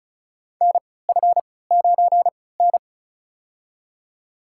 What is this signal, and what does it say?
Morse code, heard as a keyed tone, about 36 wpm, spaced as for 20 wpm, 700 Hz. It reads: NF9N
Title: USA Call Signs - 4 Characters - 3x Character Spacing - 35wpm